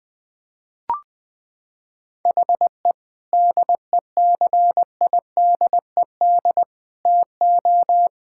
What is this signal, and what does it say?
Morse code, heard as a keyed tone, about 21 wpm, 700 Hz. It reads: HE DECIDED TO
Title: Sentences with 7-Word Phrases - 20wpm